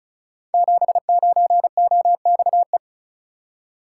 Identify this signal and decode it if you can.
Morse code, heard as a keyed tone, about 36 wpm, 690 Hz. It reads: 79OXE
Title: Code Groups - 5 Letter:Numbers - 35wpm